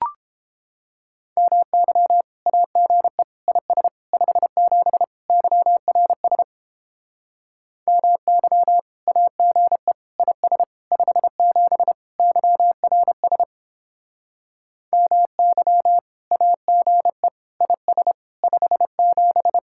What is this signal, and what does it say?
Morse code, heard as a keyed tone, about 31 wpm, 700 Hz. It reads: MY AGE IS 57 YRS MY AGE IS 57 YRS MY AGE IS 57